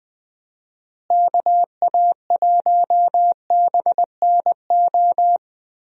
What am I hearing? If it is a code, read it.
Morse code, heard as a keyed tone, about 21 wpm, 700 Hz. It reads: KA1BNO
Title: USA Call Signs - 6 Characters - Repeated 3x - 20wpm